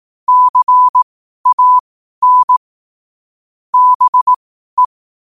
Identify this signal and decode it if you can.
Morse code, heard as a keyed tone, about 18 wpm, spaced as for 12 wpm, 1000 Hz. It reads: CAN BE
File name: qotdfiles / 12wpm